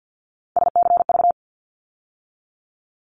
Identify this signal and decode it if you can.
Morse code, heard as a keyed tone, about 52 wpm, 700 Hz. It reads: SCV
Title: ARRL Sections 50wpm